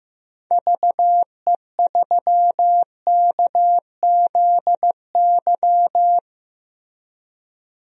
Morse code, heard as a keyed tone, about 15 wpm, 700 Hz. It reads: VE3KZY
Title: DXCC-100 Call Signs - 6 Characters - Repeated 2x - 15wpm